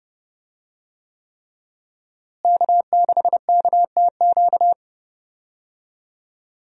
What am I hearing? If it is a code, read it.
Morse code, heard as a keyed tone, about 30 wpm, 700 Hz. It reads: K6KTQ
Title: USA Call Signs - 5 Characters - Repeated 3x - 30wpm